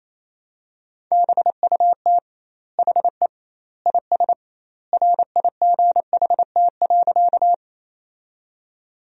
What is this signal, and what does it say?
Morse code, heard as a keyed tone, about 28 wpm, 700 Hz. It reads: BUT HE IS RIGHT.
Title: 2x Word-Spacing - Sentences from Top 200 Words - 28wpm